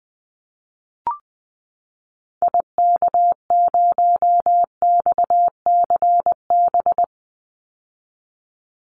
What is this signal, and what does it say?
Morse code, heard as a keyed tone, about 20 wpm, 700 Hz. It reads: IK0XCB
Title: Call Signs - 20wpm